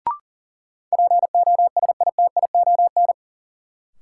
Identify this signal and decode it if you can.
Morse code, heard as a keyed tone, about 39 wpm, 700 Hz. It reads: POSITION
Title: Sentences from Top 900 Words - New Word Drill - 40wpm